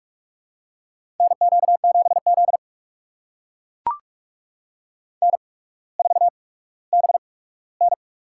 Morse code, heard as a keyed tone, about 44 wpm, spaced as for 19 wpm, 700 Hz. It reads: NQ7Z N4BN
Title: USA Call Signs - 4 Characters - 8x Character Spacing - 45wpm